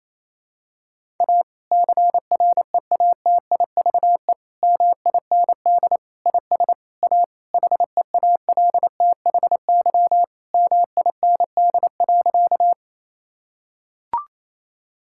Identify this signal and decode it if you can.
Morse code, heard as a keyed tone, about 28 wpm, 700 Hz. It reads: A CREATIVE MIND IS A HEALTHY MIND.
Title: Sentences from Top 1000 Words - Review Part 5 - 28wpm